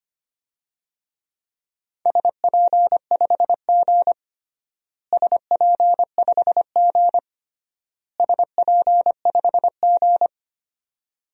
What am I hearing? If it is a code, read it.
Morse code, heard as a keyed tone, about 25 wpm, 700 Hz. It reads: SP5G SP5G SP5G
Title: DXCC-100 Call Signs - 4 Characters - Repeated 3x - 25wpm